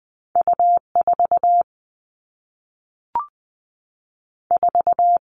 Morse code, heard as a keyed tone, about 20 wpm, 700 Hz. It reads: U 4 4
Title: All Cut Numbers - 2-Digit 20wpm